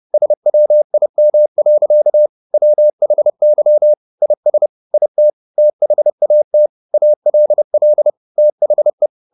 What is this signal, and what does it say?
Morse code, heard as a keyed tone, about 30 wpm, 600 Hz. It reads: SWIM. WHY IS IT THAT ALL THE
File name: the old man and the sea 2 30wpm